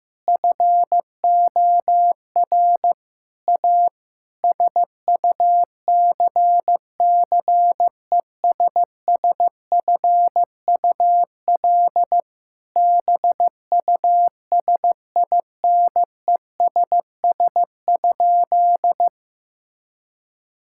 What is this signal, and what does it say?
Morse code, heard as a keyed tone, about 16 wpm, 700 Hz. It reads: FOR A SUCCESSFUL BUSINESS?
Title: Sentences from Top 1000 Words - Review Part 6 - 15wpm